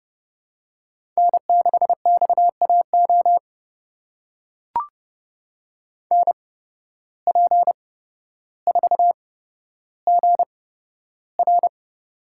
Morse code, heard as a keyed tone, about 31 wpm, spaced as for 13 wpm, 700 Hz. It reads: N6XAO NP4GR
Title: USA Call Signs - 5 Characters - 8x Character Spacing - 30wpm